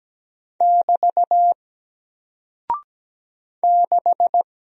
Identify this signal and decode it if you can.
Morse code, heard as a keyed tone, about 17 wpm, 700 Hz. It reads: <BT> 6
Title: Single Character (Letter:Number:Punctuation:Prosign) Alt 17wpm